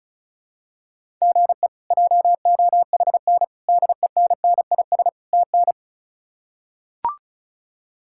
Morse code, heard as a keyed tone, about 34 wpm, 700 Hz. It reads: GE JOHN DENNIS TN
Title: K1USN SST QSO Practice - 35wpm